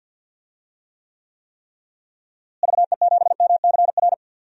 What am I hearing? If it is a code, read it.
Morse code, heard as a keyed tone, about 46 wpm, 700 Hz. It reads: VE7MCR
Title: DXCC-100 Call Signs - 6 Characters - Repeated 3x - 50wpm